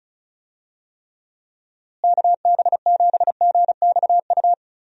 Morse code, heard as a keyed tone, about 36 wpm, 700 Hz. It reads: KB7GXU
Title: USA Call Signs - 6 Characters - Repeated 3x - 35wpm